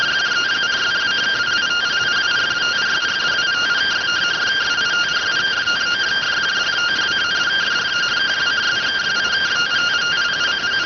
Two-tone teleprinter signal, about 170 Hz shift.